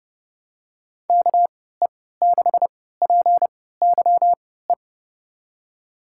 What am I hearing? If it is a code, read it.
Morse code, heard as a keyed tone, about 30 wpm, spaced as for 17 wpm, 700 Hz. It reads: KE6PYE
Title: USA Call Signs - 6 Characters - 3x Character Spacing - 30wpm